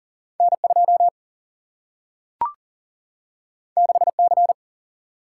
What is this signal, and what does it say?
Morse code, heard as a keyed tone, about 39 wpm, 700 Hz. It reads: N 2 6 C
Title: Single-Letter Single-Number 40wpm